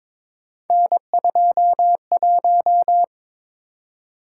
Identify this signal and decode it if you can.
Morse code, heard as a keyed tone, about 23 wpm, 700 Hz. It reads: N21
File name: practice / 22wpm / Three Letter-Number Combinations - 22wpm